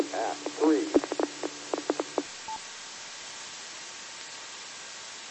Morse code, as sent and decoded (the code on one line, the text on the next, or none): none